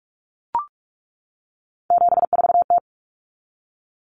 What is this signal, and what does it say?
Morse code, heard as a keyed tone, about 44 wpm, 700 Hz. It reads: 74T
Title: Common Cut Numbers - 3-Digit 45wpm